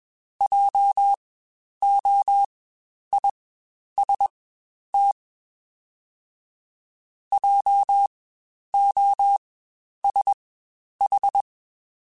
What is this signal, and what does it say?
Morse code, heard as a keyed tone, about 20 wpm, spaced as for 10 wpm, 800 Hz. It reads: JOIST JOSH
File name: Spaced Alphas / Spaced Alphas mp3s / Js sp @20